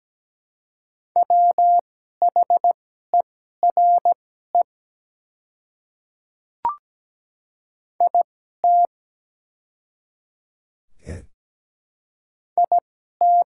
Morse code, heard as a keyed tone, about 17 wpm, spaced as for 12 wpm, 700 Hz. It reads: WHERE IT IT
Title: 2x Character Spacing - Top 200 Words - 17wpm